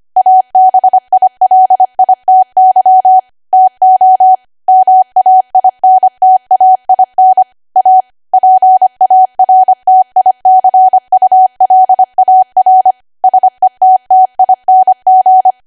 Morse code, heard as a keyed tone, about 25 wpm, 750 Hz. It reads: ABILITY TO MAINTAIN A PARTICULAR SETTING
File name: cwa / W1AW-Level3 / 25WPM